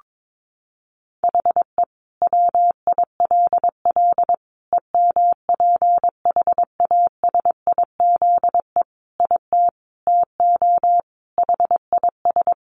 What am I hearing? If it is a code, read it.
Morse code, heard as a keyed tone, about 22 wpm, 700 Hz. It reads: HE WILL EMPHASIZE IT TO HIS
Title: Sentences from Top 1500 Words 22wpm